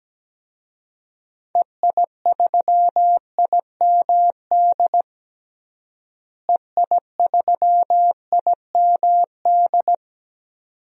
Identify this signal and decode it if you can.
Morse code, heard as a keyed tone, about 17 wpm, 700 Hz. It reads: EI3IMD EI3IMD
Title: DXCC-100 Call Signs - 6 Characters - Repeated 2x - 17wpm